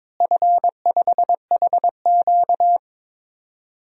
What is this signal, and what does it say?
Morse code, heard as a keyed tone, about 22 wpm, 700 Hz. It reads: F5HQ
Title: DXCC-100 Call Signs - 4 Characters - Repeated 2x - 22wpm